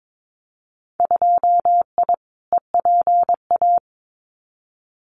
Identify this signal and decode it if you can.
Morse code, heard as a keyed tone, about 22 wpm, 700 Hz. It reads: 2I EPA
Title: Winter Field Day Exchanges Encore 22wpm